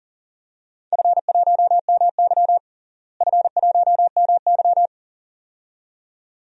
Morse code, heard as a keyed tone, about 40 wpm, 700 Hz. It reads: F1MY F1MY
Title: DXCC-100 Call Signs - 4 Characters - Repeated 2x - 40wpm